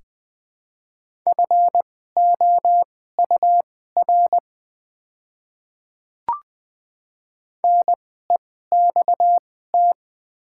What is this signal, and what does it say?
Morse code, heard as a keyed tone, about 20 wpm, spaced as for 14 wpm, 700 Hz. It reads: FOUR NEXT
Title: 2x Character Spacing - Top 500 Words - 20wpm